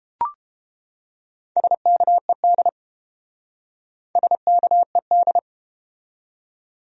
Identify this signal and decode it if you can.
Morse code, heard as a keyed tone, about 30 wpm, 700 Hz. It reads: SKED SKED